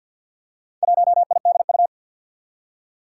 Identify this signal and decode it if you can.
Morse code, heard as a keyed tone, about 53 wpm, 700 Hz. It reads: 1IDU
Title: Four Challenging Letter-Number Combinations 50wpm